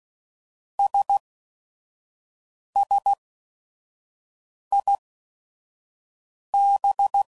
Morse code, heard as a keyed tone, about 16 wpm, spaced as for 5 wpm, 800 Hz. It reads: SSIB
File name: CW Code Groups @5WPM / Technical Subjects / AND Gate